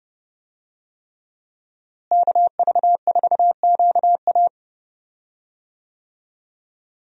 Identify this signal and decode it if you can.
Morse code, heard as a keyed tone, about 30 wpm, 700 Hz. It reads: KV4QA
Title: USA Call Signs - 5 Characters - Repeated 3x - 30wpm